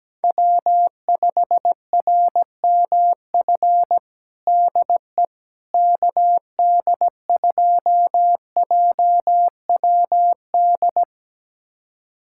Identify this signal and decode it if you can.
Morse code, heard as a keyed tone, about 17 wpm, 700 Hz. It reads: W5RMF DE KD2JWD
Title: USA Callsign DE Callsign 17wpm